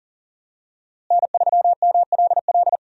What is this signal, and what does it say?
Morse code, heard as a keyed tone, about 40 wpm, 700 Hz. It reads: N3MLL
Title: USA Call Signs - 5 Characters - Repeated 3x - 40wpm